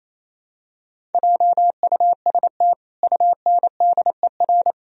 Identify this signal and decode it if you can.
Morse code, heard as a keyed tone, about 28 wpm, 700 Hz. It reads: JUST UNDER
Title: Sentences from Top 1000 Words - Review Part 1 - 28wpm